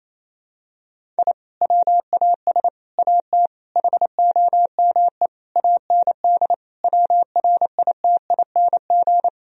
Morse code, heard as a keyed tone, about 28 wpm, 700 Hz. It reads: I WAS AT HOME AND WRITING
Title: Sentences from Top 1000 Words - Review Part 1 - 28wpm